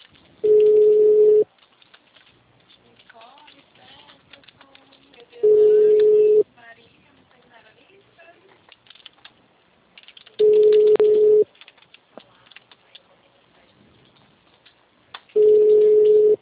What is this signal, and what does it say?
Ringback tone (Europe, 425 Hz), with speech.
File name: out- -2028- - - .8590